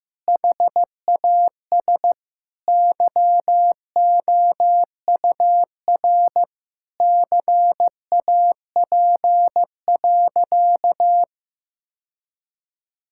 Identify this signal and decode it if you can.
Morse code, heard as a keyed tone, about 15 wpm, 700 Hz. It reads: HAS YOUR CAP.
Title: Sentences from Top 1600 Words 15wpm